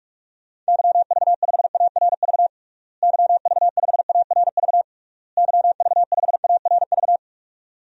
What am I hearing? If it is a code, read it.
Morse code, heard as a keyed tone, about 42 wpm, 700 Hz. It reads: YV5ARV YV5ARV YV5ARV
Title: DXCC-100 Call Signs - 6 Characters - Repeated 3x - 45wpm